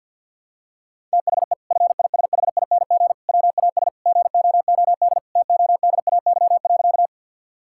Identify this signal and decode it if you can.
Morse code, heard as a keyed tone, about 48 wpm, 700 Hz. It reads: THE FISHING WAS GOOD TODAY.
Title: Sentences from Top 1000 Words - Review Part 5 - 50wpm